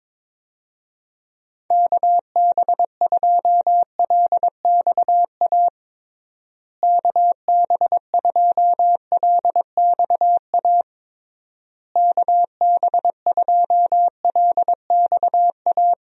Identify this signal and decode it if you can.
Morse code, heard as a keyed tone, about 22 wpm, 700 Hz. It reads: KB2LXA KB2LXA KB2LXA
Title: USA Call Signs - 6 Characters - Repeated 3x - 22wpm